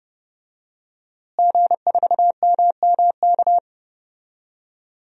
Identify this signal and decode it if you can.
Morse code, heard as a keyed tone, about 30 wpm, 700 Hz. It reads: G4MMK
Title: DXCC-100 Call Signs - 5 Characters - Repeated 3x - 30wpm